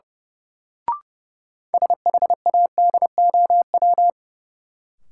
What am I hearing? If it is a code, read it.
Morse code, heard as a keyed tone, about 30 wpm, 700 Hz. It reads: SHADOW
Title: Sets of 5 Words Encore - New Word Drill - 30wpm